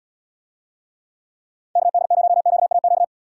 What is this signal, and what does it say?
Morse code, heard as a keyed tone, about 74 wpm, 700 Hz. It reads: KD0QAQ